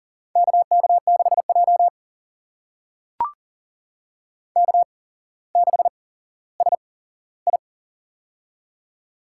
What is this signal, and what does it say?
Morse code, heard as a keyed tone, about 40 wpm, spaced as for 17 wpm, 700 Hz. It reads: KK6J K6SI
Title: USA Call Signs - 4 Characters - 8x Character Spacing - 40wpm